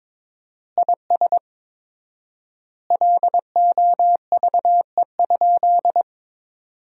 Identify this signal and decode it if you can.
Morse code, heard as a keyed tone, about 22 wpm, 700 Hz. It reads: IS LOVE?
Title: 4x Word-Spacing - Sentences from Top 400 Words - 22wpm